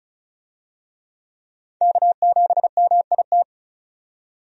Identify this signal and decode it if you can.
Morse code, heard as a keyed tone, about 36 wpm, 700 Hz. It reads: K7MIT
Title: USA Call Signs - 5 Characters - 35wpm